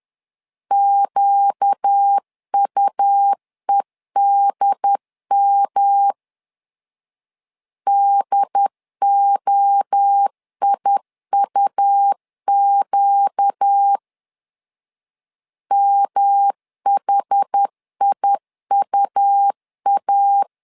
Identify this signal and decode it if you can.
Morse code, heard as a keyed tone, about 10 wpm, 800 Hz. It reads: QUEDM DOIUQ MHIUA
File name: LEZIONI / 4 Q 10wpm 2210